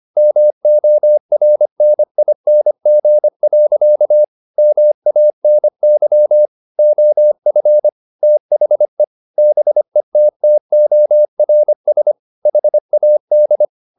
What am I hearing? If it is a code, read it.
Morse code, heard as a keyed tone, about 25 wpm, 600 Hz. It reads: MORNING. MANY OF THE BETTORS HAD